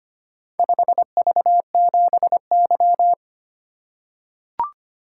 Morse code, heard as a keyed tone, about 25 wpm, 700 Hz. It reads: 5V7Y
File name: practice / 25wpm / Four Letter-Number Combinations - 25wpm